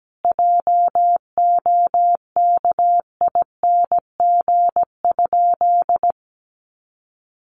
Morse code, heard as a keyed tone, about 17 wpm, 700 Hz. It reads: JOKING?
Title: A Sound Of Thunder 17wpm